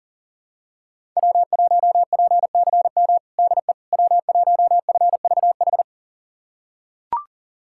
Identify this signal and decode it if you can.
Morse code, heard as a keyed tone, about 37 wpm, 700 Hz. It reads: W1PCM DE W1FVH
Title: USA Callsign DE Callsign - 5 Characters - 40wpm